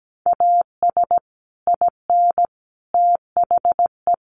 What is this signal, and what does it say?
Morse code, heard as a keyed tone, about 17 wpm, 700 Hz. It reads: AS IN THE
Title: Sets of 5 Words 17wpm